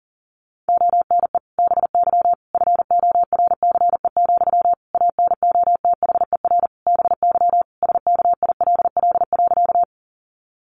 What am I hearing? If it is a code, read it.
Morse code, heard as a keyed tone, about 40 wpm, 700 Hz. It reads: ONE BY FORCE, ANOTHER BY SKILL.
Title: Sentences from Top 1200 Words 40wpm